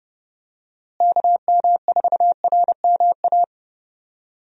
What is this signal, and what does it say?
Morse code, heard as a keyed tone, about 32 wpm, 700 Hz. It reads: KM4RMA
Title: USA Call Signs - 6 Characters - Repeated 2x - 30wpm